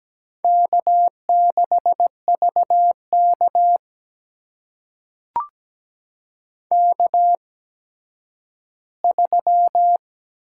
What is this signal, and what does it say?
Morse code, heard as a keyed tone, about 17 wpm, spaced as for 10 wpm, 700 Hz. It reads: K6VK K3
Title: USA Call Signs - 4 Characters - 8x Character Spacing - 17wpm